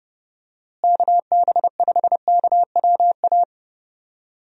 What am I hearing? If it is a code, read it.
Morse code, heard as a keyed tone, about 30 wpm, 700 Hz. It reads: KB5KWA
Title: USA Call Signs - 6 Characters - Repeated 3x - 30wpm